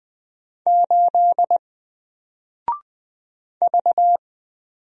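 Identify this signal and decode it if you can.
Morse code, heard as a keyed tone, about 20 wpm, 700 Hz. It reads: 8 V
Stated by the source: Single letter and number Morse drill at 20 wpm, from N4LCD.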